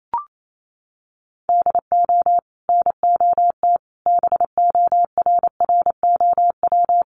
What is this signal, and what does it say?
Morse code, heard as a keyed tone, about 28 wpm, 700 Hz. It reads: DO NOT BORROW
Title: Sentences from Top 1500 Words 28wpm